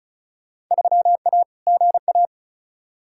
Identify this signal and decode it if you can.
Morse code, heard as a keyed tone, about 34 wpm, 700 Hz. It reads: 3A GA
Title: Field Day Exchanges - 35wpm